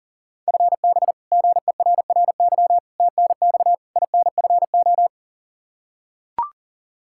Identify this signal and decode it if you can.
Morse code, heard as a keyed tone, about 40 wpm, 700 Hz. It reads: FB GERRY TNX INFO
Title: Practice QSO - Exchange 3 - More Info - 40wpm